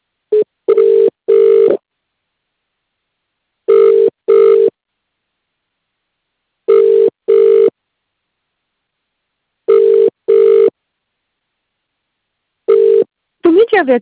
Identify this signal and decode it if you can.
Ringback tone (UK), 400 + 450 Hz, with speech.